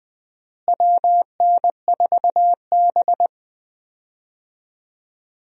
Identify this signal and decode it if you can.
Morse code, heard as a keyed tone, about 20 wpm, 700 Hz. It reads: WN4B
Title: USA Call Signs - 4 Characters - Repeated 3x - 20wpm